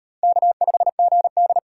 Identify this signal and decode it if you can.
Morse code, heard as a keyed tone, about 38 wpm, 700 Hz. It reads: K5GD